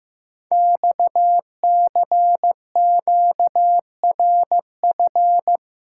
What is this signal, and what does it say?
Morse code, heard as a keyed tone, about 15 wpm, 700 Hz. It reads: XCQRF
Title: Code Groups - 5 Letters - 15wpm